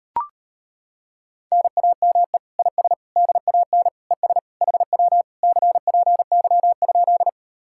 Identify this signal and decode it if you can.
Morse code, heard as a keyed tone, about 33 wpm, 700 Hz. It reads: NAME IS DAN ES HW CPY?